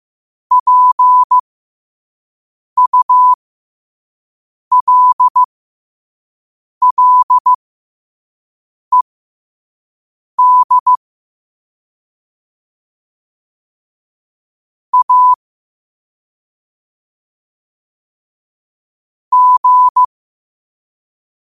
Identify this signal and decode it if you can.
Morse code, heard as a keyed tone, about 15 wpm, spaced as for 5 wpm, 1000 Hz. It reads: PULLED A G
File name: qotdfiles / 5wpm